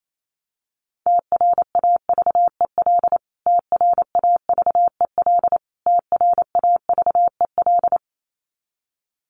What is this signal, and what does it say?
Morse code, heard as a keyed tone, about 28 wpm, 700 Hz. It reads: TRAVEL TRAVEL TRAVEL
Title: Top 500 Words Repeated 28wpm